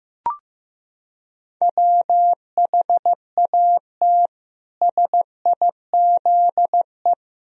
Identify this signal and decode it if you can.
Morse code, heard as a keyed tone, about 15 wpm, 700 Hz. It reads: WHAT SIZE
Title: Sentences from Top 1000 Words - Review Part 2 - 15wpm